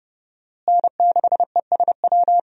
Morse code, heard as a keyed tone, about 32 wpm, 700 Hz. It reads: N6ESW
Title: USA Call Signs - 5 Characters - Repeated 2x - 30wpm